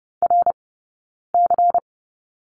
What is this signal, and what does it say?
Morse code, heard as a keyed tone, about 30 wpm, 700 Hz. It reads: R C
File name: Lesson 19d - Letter w - Two Characters 30wpm